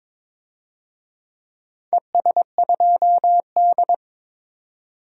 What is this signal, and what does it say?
Morse code, heard as a keyed tone, about 22 wpm, 700 Hz. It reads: ES2D
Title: DXCC-100 Call Signs - 4 Characters - 22wpm